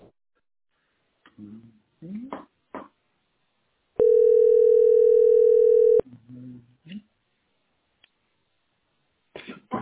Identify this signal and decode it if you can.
Ringback tone (North America), 440 + 480 Hz, with speech.
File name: 6223af3e-3cda-43c8-a2df